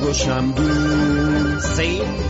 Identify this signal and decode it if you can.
Speech. There is no code.